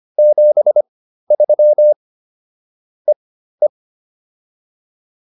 Morse code, heard as a keyed tone, about 25 wpm, spaced as for 13 wpm, 600 Hz. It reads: 73 EE